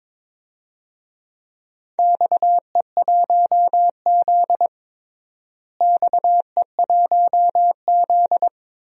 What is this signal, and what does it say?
Morse code, heard as a keyed tone, about 22 wpm, 700 Hz. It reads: XE1Z XE1Z
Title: DXCC-100 Call Signs - 4 Characters - Repeated 3x - 22wpm